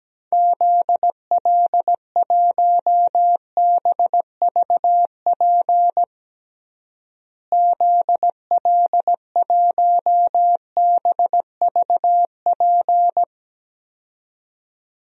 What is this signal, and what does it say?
Morse code, heard as a keyed tone, about 17 wpm, 700 Hz. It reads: ZL1BVP ZL1BVP
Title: DXCC-100 Call Signs - 6 Characters - Repeated 2x - 17wpm